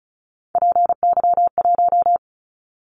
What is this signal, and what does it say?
Morse code, heard as a keyed tone, about 34 wpm, 700 Hz. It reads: PY1
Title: Callsign Prefixes for Top 100 DXCC - 35wpm